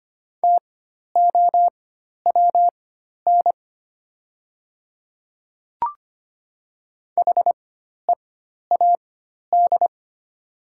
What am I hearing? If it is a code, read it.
Morse code, heard as a keyed tone, about 25 wpm, spaced as for 12 wpm, 700 Hz. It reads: TOWN HEAD
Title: 4x Character Spacing - Top 500 Words - 25wpm